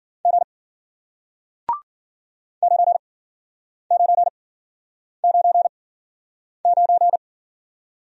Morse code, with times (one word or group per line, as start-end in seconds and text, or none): 0.2-0.4 G
2.6-7.2 9999